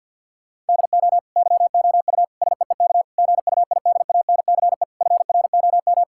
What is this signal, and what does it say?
Morse code, heard as a keyed tone, about 50 wpm, 700 Hz. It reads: DO YOU SEEK GUIDANCE FROM
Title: Sentences from Top 1600 Words 50wpm